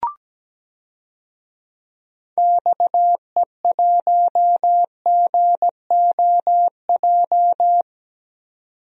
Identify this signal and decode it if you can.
Morse code, heard as a keyed tone, about 17 wpm, 700 Hz. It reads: XE1GOJ
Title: DXCC-100 Call Signs - 6 Characters - 17wpm